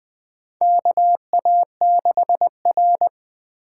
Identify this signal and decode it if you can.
Morse code, heard as a keyed tone, about 20 wpm, 700 Hz. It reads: KA6R
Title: USA Call Signs - 4 Characters - 4x Character Spacing - 20wpm